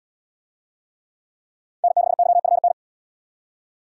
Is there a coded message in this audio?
Yes: Morse code from 1.8 to 2.7 s, heard as a keyed tone, about 73 wpm, 700 Hz. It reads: N6YFN